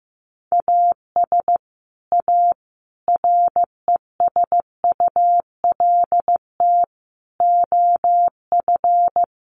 Morse code, heard as a keyed tone, about 15 wpm, 700 Hz. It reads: AS A RESULT OF
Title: Sets of 5 Words 15wpm